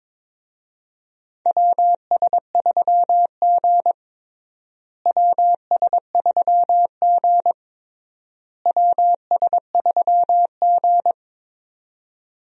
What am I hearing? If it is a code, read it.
Morse code, heard as a keyed tone, about 22 wpm, 700 Hz. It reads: WS3G WS3G WS3G
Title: USA Call Signs - 4 Characters - Repeated 3x - 22wpm